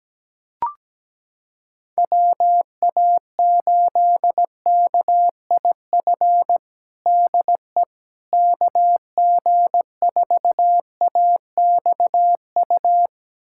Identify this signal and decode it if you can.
Morse code, heard as a keyed tone, about 17 wpm, 700 Hz. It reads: WA8KIF DE KG4AXU
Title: USA Callsign DE Callsign - 6 Characters - 17wpm